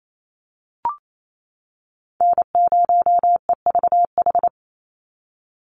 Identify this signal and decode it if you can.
Morse code, heard as a keyed tone, about 28 wpm, 700 Hz. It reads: N0EVH
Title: NA SOTA Chaser Callsigns 28wpm